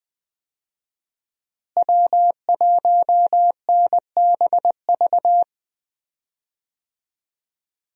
Morse code, heard as a keyed tone, about 21 wpm, 700 Hz. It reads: W1NBV
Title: USA Call Signs - 5 Characters - Repeated 2x - 20wpm